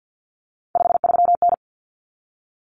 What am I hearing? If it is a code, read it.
Morse code, heard as a keyed tone, about 50 wpm, 700 Hz. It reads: 53N
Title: Common Cut Numbers - 3-Digit 50wpm